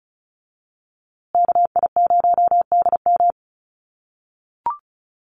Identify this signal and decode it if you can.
Morse code, heard as a keyed tone, about 34 wpm, 700 Hz. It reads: KI0DM
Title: USA Call Signs Repeated 3x 35wpm